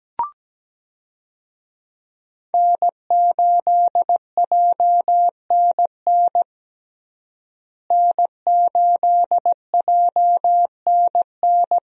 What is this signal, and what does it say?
Morse code, heard as a keyed tone, about 17 wpm, 700 Hz. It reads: N8JNN N8JNN
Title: USA Call Signs - 5 Characters - Repeated 3x - 17wpm